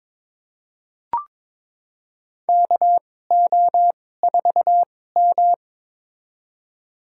Morse code, heard as a keyed tone, about 22 wpm, spaced as for 16 wpm, 700 Hz. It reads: KO4M
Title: USA Call Signs - 4 Characters - 2x Character Spacing - 22wpm